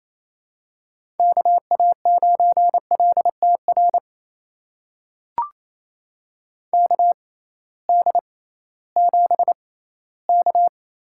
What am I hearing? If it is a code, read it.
Morse code, heard as a keyed tone, about 28 wpm, spaced as for 16 wpm, 700 Hz. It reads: KA9LTR KD7K
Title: USA Call Signs - 6 Characters - 6x Character Spacing - 28wpm